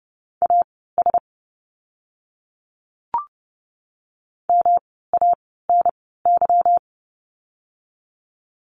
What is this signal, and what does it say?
Morse code, heard as a keyed tone, about 30 wpm, spaced as for 17 wpm, 700 Hz. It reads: AS MANY
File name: course / 30:10wpm / 230 Lesson 37f - Letter x - Words 30wpm